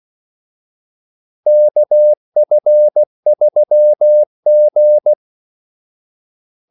Morse code, heard as a keyed tone, about 16 wpm, 600 Hz. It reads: KF3G
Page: Kaldesignaler 16 wpm